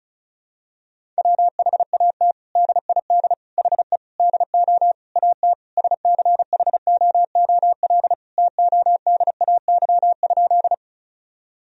Morse code, heard as a keyed tone, about 34 wpm, 700 Hz. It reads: WHAT DID HE DO AT SCHOOL TODAY?
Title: Sentences from Top 1000 Words - Review Part 1 - 35wpm